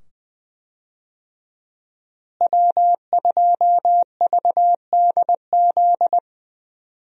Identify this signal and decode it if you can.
Morse code, heard as a keyed tone, about 21 wpm, 700 Hz. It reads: W2VDZ
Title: USA Call Signs - 5 Characters - Repeated 2x - 20wpm